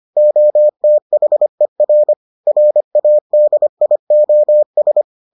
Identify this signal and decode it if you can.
Morse code, heard as a keyed tone, about 25 wpm, 600 Hz. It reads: OTHER RADIOS